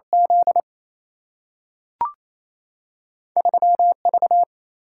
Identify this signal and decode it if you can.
Morse code, heard as a keyed tone, about 28 wpm, 700 Hz. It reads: Z 3 V
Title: Single-Letter Single-Number 28wpm